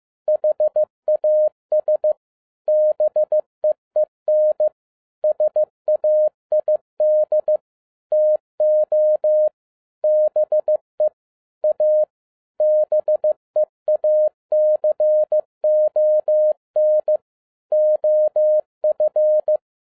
Morse code, heard as a keyed tone, about 15 wpm, 610 Hz. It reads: HAS BEEN SAID TO BE A BEACON OF
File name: 2011-07-mp3 / n47 JOHN K3TN 15wpm